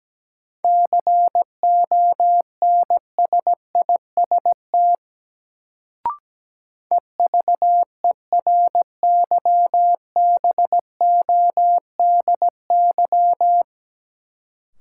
Morse code, heard as a keyed tone, about 17 wpm, 700 Hz. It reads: CONSIST EVERYBODY
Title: Sentences from Top 1400 Words - New Word Drill Alt - 17wpm